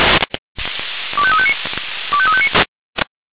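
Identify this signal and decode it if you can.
Five-tone selective call (ZVEI), heard as quick runs of five tones, tones of about 70 ms. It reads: 25359 25359
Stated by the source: ZVEI five-tone selective call received on a software-defined radio.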